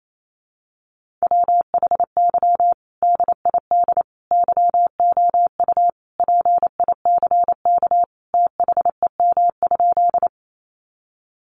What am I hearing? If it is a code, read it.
Morse code, heard as a keyed tone, about 28 wpm, 700 Hz. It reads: WHY DID YOU PICK THEM?
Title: Sentences from Top 1000 Words 28wpm